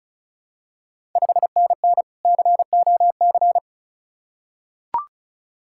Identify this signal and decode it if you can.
Morse code, heard as a keyed tone, about 36 wpm, 690 Hz. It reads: 5NN COC
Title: Ontario QSO Party Exchange 35wpm